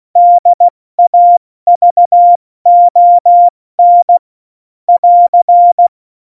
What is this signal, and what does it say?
Morse code, heard as a keyed tone, about 16 wpm, 700 Hz. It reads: DAVON <AR>
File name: Morsekurs / Witze / Witz 06 March 2018, 70 F80